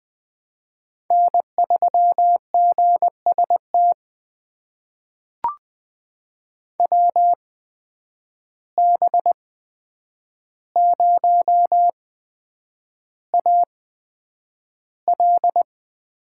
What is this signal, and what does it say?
Morse code, heard as a keyed tone, about 20 wpm, spaced as for 9 wpm, 700 Hz. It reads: N3GST WB0AL
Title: USA Call Signs - 5 Characters - 8x Character Spacing - 20wpm